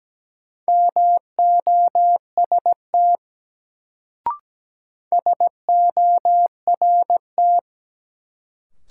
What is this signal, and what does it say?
Morse code, heard as a keyed tone, about 17 wpm, 700 Hz. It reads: MOST SORT
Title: Sets of 2 Words - New Word Drill - 17wpm